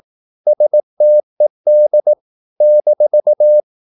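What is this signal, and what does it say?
Morse code, heard as a keyed tone, about 18 wpm, 600 Hz. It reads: STED -
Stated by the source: Slow (QRS) Morse practice at 18 wpm, CW med Gnister.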